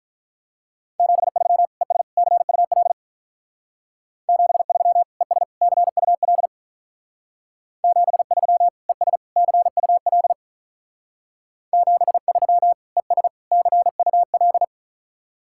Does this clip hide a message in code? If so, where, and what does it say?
Yes: Morse code from 1.0 to 14.6 s, heard as a keyed tone, about 44 wpm, 700 Hz. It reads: 73 ES CUL 73 ES CUL 73 ES CUL 73 ES CUL